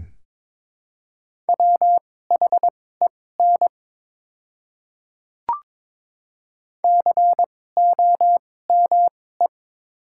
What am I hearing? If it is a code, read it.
Morse code, heard as a keyed tone, about 22 wpm, spaced as for 16 wpm, 700 Hz. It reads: WHEN COME
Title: 2x Character Spacing - Top 100 Words - 22wpm